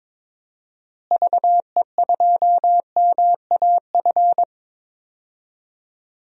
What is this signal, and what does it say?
Morse code, heard as a keyed tone, about 22 wpm, 700 Hz. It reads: VE2MAF
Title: DXCC-100 Call Signs - 6 Characters - Repeated 3x - 22wpm